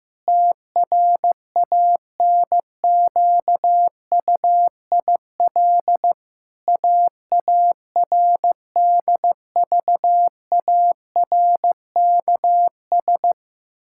Morse code, heard as a keyed tone, about 15 wpm, 700 Hz. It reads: TRANQUIL AARDVARKS
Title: Pangram Challenge v2 - 15wpm